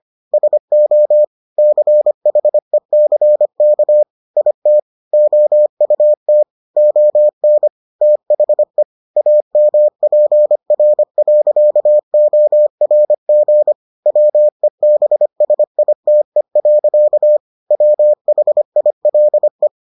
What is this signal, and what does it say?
Morse code, heard as a keyed tone, about 25 wpm, 600 Hz. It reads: SO CHECK IT OUT ON THE AMPR.ORG WEBSITE. WHILE